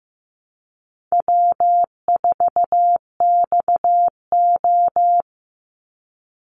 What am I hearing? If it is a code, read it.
Morse code, heard as a keyed tone, about 15 wpm, 700 Hz. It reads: W4XO
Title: Easy Call Signs Repeated 3x - 15wpm